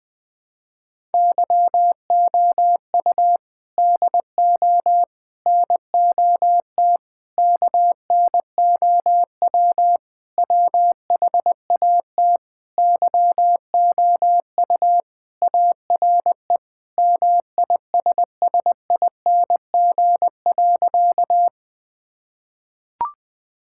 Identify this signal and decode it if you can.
Morse code, heard as a keyed tone, about 21 wpm, 700 Hz. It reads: YOU DO NOT KNOW WHAT YOU ARE MISSING.
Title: Sentences with 5-Word Phrases - 20wpm